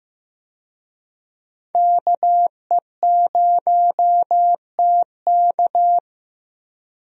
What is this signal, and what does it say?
Morse code, heard as a keyed tone, about 15 wpm, 700 Hz. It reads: KE0TK
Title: USA Call Signs - 5 Characters - 15wpm